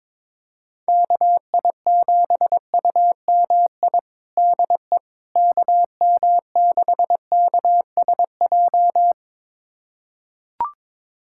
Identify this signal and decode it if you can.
Morse code, heard as a keyed tone, about 22 wpm, 700 Hz. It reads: KI7UMI DE KM6KSJ
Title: USA Callsign DE Callsign - 6 Characters - 22wpm